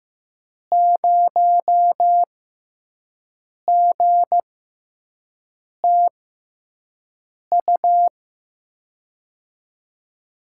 Morse code, heard as a keyed tone, about 15 wpm, spaced as for 5 wpm, 700 Hz. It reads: 0GTU
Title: USA Call Signs - 6 Characters - 6x Character Spacing - 15wpm